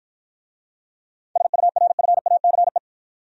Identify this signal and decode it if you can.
Morse code, heard as a keyed tone, about 50 wpm, 700 Hz. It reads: SURFACE